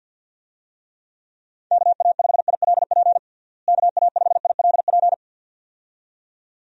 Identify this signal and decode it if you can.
Morse code, heard as a keyed tone, about 51 wpm, 700 Hz. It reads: KA5ILP KA5ILP
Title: USA Call Signs - 6 Characters - Repeated 2x - 50wpm